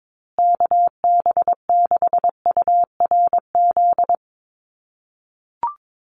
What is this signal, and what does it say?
Morse code, heard as a keyed tone, about 22 wpm, 700 Hz. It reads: KB6URZ
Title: USA Call Signs Repeated 3x 22wpm